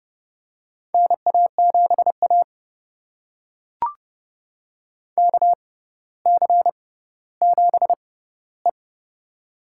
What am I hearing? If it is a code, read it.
Morse code, heard as a keyed tone, about 32 wpm, spaced as for 16 wpm, 700 Hz. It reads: NA7A KC7E
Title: USA Call Signs - 4 Characters - 6x Character Spacing - 30wpm